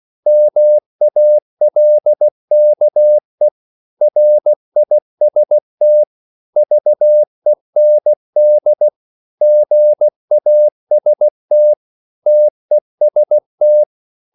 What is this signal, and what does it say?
Morse code, heard as a keyed tone, about 16 wpm, 600 Hz. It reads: MALKE RIST VEND GAST TEST
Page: Korte danske ord 16wpm